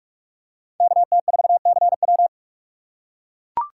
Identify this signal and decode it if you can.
Morse code, heard as a keyed tone, about 44 wpm, 700 Hz. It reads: KT4CW